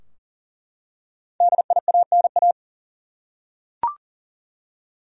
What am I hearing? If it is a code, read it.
Morse code, heard as a keyed tone, about 40 wpm, 700 Hz. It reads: DIANA
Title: USA Common Names - Max 5 Letters - 40wpm